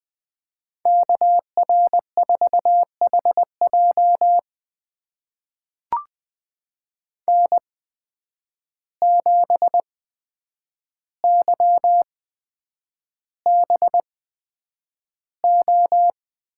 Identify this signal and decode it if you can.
Morse code, heard as a keyed tone, about 20 wpm, spaced as for 9 wpm, 700 Hz. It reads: KR4HJ N7YBO
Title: USA Call Signs - 5 Characters - 8x Character Spacing - 20wpm